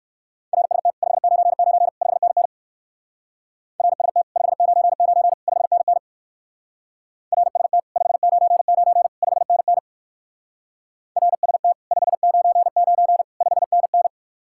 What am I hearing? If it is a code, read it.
Morse code, heard as a keyed tone, about 55 wpm, 700 Hz. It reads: RST 599 5NN RST 599 5NN RST 599 5NN RST 599 5NN
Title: Speed-Racing Common QSO Phrases - 45wpm